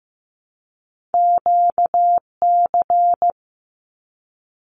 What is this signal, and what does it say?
Morse code, heard as a keyed tone, about 15 wpm, 700 Hz. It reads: QC
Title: Canadian Provinces and Territories 15wpm